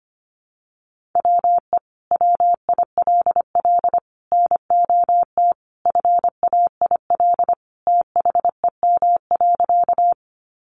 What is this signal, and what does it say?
Morse code, heard as a keyed tone, about 25 wpm, 700 Hz. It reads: WE WILL NOT FAIL THEM.
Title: Sentences from Top 1100 Words 25wpm